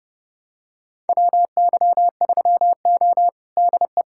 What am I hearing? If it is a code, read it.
Morse code, heard as a keyed tone, about 32 wpm, 700 Hz. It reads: WY3O DE
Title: USA Callsign DE Callsign - 4 Characters - 30wpm